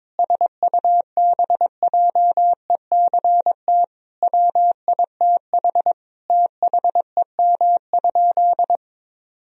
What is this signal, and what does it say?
Morse code, heard as a keyed tone, about 22 wpm, 700 Hz. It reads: SUBJECT WITH THEM?
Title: Sentences from Top 1000 Words - Review Part 1 - 22wpm